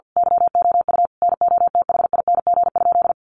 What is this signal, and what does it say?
Morse code, heard as a keyed tone, about 50 wpm, 700 Hz. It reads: YOU NOTHING?
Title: Sentences from Top 500 Words 50wpm